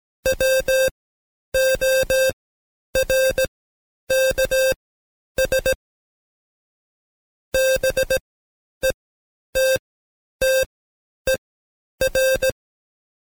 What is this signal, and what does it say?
Morse code, heard as a keyed tone, about 17 wpm, spaced as for 9 wpm, 530 Hz. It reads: WORKS BETTER
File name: Code Quick / AudioDownld2 / Seg131-Stepper4 9WPM